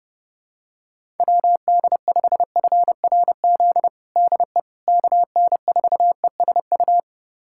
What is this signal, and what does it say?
Morse code, heard as a keyed tone, about 28 wpm, 700 Hz. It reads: WD5FRZ DE KN4ESU
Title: USA Callsign DE Callsign 30wpm